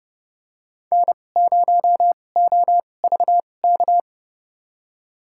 Morse code, heard as a keyed tone, about 32 wpm, spaced as for 22 wpm, 700 Hz. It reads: N0OVK
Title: USA Call Signs - 2x Character Spacing - 30wpm